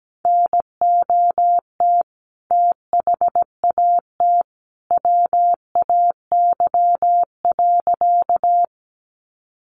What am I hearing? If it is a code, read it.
Morse code, heard as a keyed tone, about 17 wpm, 700 Hz. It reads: NOT THAT WAY.
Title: A Sound Of Thunder 17wpm